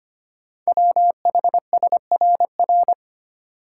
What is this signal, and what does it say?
Morse code, heard as a keyed tone, about 25 wpm, 700 Hz. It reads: WHSRR
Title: Code Groups - 5 Letters - 25wpm